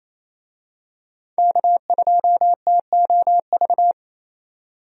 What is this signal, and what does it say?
Morse code, heard as a keyed tone, about 28 wpm, 700 Hz. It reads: K2TOV
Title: USA Call Signs - 5 Characters - 28wpm